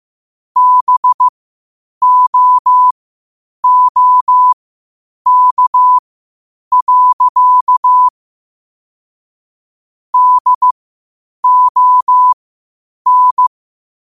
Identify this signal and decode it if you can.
Morse code, heard as a keyed tone, about 15 wpm, spaced as for 8 wpm, 1000 Hz. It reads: BOOK. DON
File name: qotdfiles / 8wpm